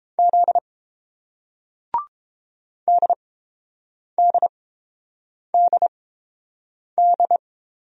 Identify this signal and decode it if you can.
Morse code, heard as a keyed tone, about 30 wpm, spaced as for 7 wpm, 700 Hz. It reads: Z DDDD